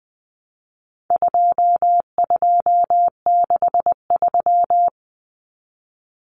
Morse code, heard as a keyed tone, about 20 wpm, 700 Hz. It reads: 2263
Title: Common Cut Numbers - 4-Digit 20wpm